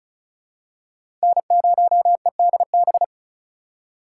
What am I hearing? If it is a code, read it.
Morse code, heard as a keyed tone, about 36 wpm, 700 Hz. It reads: N0EDB
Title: USA Call Signs - 5 Characters - 4x Character Spacing - 35wpm